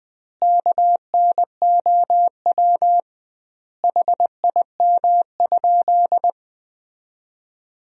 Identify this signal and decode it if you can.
Morse code, heard as a keyed tone, about 20 wpm, 700 Hz. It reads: KNOW HIM?
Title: 2x Word-Spacing - Sentences from Top 100 Words - 20wpm